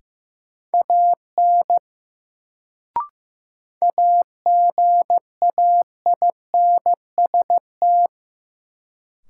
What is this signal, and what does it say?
Morse code, heard as a keyed tone, about 16 wpm, 700 Hz. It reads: AN AGAINST
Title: Sets of 2 Words - New Word Drill - 15wpm